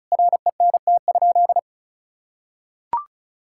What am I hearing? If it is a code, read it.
Morse code, heard as a keyed tone, about 34 wpm, 700 Hz. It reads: RENT?
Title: Sentences from Top 1000 Words - Review Part 5 - 35wpm